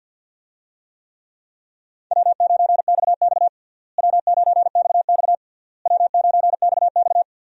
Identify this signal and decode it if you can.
Morse code, heard as a keyed tone, about 52 wpm, 700 Hz. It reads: W9XX W9XX W9XX
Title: USA Call Signs - 4 Characters - Repeated 3x - 50wpm